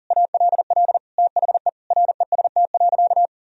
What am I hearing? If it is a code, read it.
Morse code, heard as a keyed tone, about 40 wpm, 700 Hz. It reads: ALL THE REST.
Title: Sentences from Top 1000 Words - Review Part 1 - 40wpm